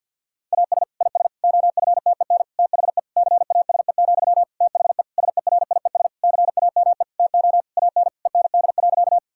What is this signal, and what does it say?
Morse code, heard as a keyed tone, about 50 wpm, 700 Hz. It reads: AS IS OFTEN THE CASE, THE SERIES CAME TO AN END.
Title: Sentences from Top 1000 Words - Review Part 5 - 50wpm